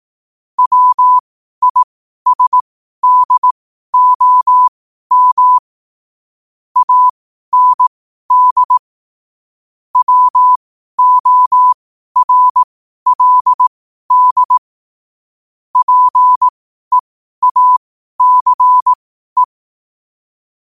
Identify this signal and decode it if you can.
Morse code, heard as a keyed tone, about 18 wpm, spaced as for 12 wpm, 1000 Hz. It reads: WISDOM AND WORLD PEACE
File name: qotdfiles / 12wpm